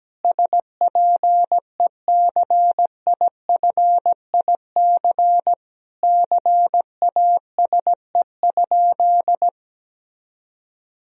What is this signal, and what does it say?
Morse code, heard as a keyed tone, about 17 wpm, 700 Hz. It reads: SPECIFIC CASE?
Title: Sentences from Top 1000 Words - Review Part 3 - 17wpm